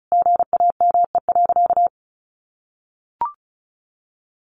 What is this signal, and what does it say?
Morse code, heard as a keyed tone, about 34 wpm, 700 Hz. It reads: GAME.
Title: Additional Easy Phrases 35wpm